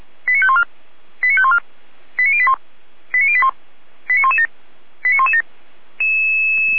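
Five-tone selective call (ZVEI), heard as quick runs of five tones, tones of about 70 ms. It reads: 89514 89514 89081 89081 89117 89117
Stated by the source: ZVEI five-tone selective call received on an SDR, channel 455.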